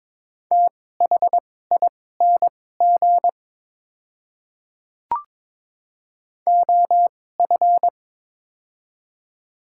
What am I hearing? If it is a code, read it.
Morse code, heard as a keyed tone, about 22 wpm, spaced as for 16 wpm, 700 Hz. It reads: THING OF
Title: 2x Character Spacing - Top 100 Words - 22wpm